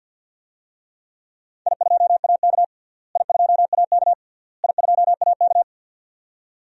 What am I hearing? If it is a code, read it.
Morse code, heard as a keyed tone, about 48 wpm, 700 Hz. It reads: I2AK I2AK I2AK
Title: DXCC-100 Call Signs - 4 Characters - Repeated 3x - 50wpm